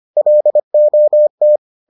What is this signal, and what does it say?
Morse code, heard as a keyed tone, about 25 wpm, 600 Hz. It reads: LOT